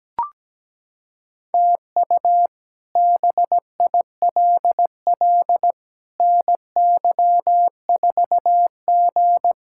Morse code, heard as a keyed tone, about 17 wpm, 700 Hz. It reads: TU BILL NY4G
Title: K1USN SST QSO Practice - 17wpm